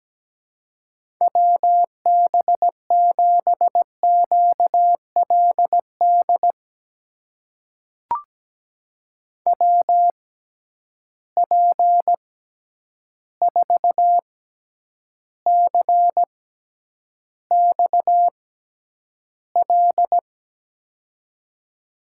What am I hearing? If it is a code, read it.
Morse code, heard as a keyed tone, about 17 wpm, spaced as for 9 wpm, 700 Hz. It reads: WB7QLD WP4CXL
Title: USA Call Signs - 6 Characters - 6x Character Spacing - 17wpm